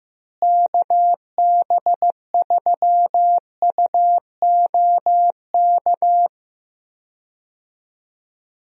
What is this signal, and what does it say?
Morse code, heard as a keyed tone, about 15 wpm, 700 Hz. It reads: KB3UOK
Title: USA Call Signs - 6 Characters - Repeated 3x - 15wpm